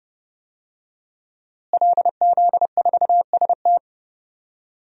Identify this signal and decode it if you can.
Morse code, heard as a keyed tone, about 31 wpm, 700 Hz. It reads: LZ4ST
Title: DXCC-100 Call Signs - 5 Characters - Repeated 2x - 30wpm